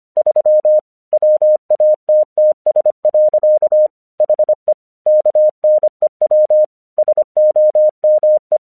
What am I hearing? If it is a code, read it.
Morse code, heard as a keyed tone, about 25 wpm, 610 Hz. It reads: 3 WATTS. HE KNEW SOME
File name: 1 English 25wpm / AQC32